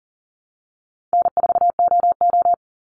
Morse code, heard as a keyed tone, about 40 wpm, 700 Hz. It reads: N4OO
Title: Easy Call Signs Repeated 3x - 40wpm